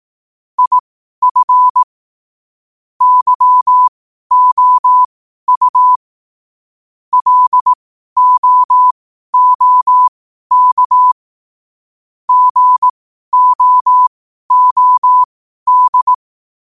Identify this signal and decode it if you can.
Morse code, heard as a keyed tone, about 18 wpm, spaced as for 12 wpm, 1000 Hz. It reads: IF YOU LOOK GOOD
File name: qotdfiles / 12wpm